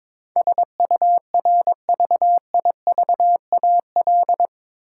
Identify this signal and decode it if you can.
Morse code, heard as a keyed tone, about 21 wpm, 700 Hz. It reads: SURVIVAL
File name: practice / 22wpm / Eight Letter Words - 22wpm